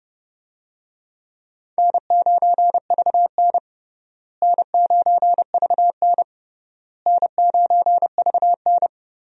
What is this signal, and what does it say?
Morse code, heard as a keyed tone, about 31 wpm, 700 Hz. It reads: N9VN N9VN N9VN
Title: USA Call Signs - 4 Characters - Repeated 3x - 30wpm